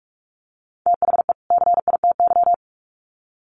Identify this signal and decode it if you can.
Morse code, heard as a keyed tone, about 44 wpm, 700 Hz. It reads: THE CITY
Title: Sets of 2 Words 45wpm